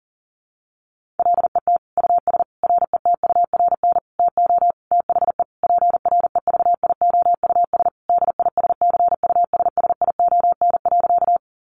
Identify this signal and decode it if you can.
Morse code, heard as a keyed tone, about 37 wpm, 700 Hz. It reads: LET US RETURN TO THE PREVIOUS DISCUSSION.
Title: Sentences from Top 1500 Words 40wpm